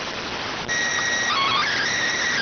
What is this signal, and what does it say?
SSTV (slow-scan television), PD 290.